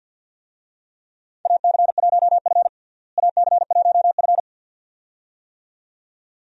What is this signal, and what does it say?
Morse code, heard as a keyed tone, about 46 wpm, 700 Hz. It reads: AC1F AC1F
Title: USA Call Signs - 4 Characters - Repeated 2x - 50wpm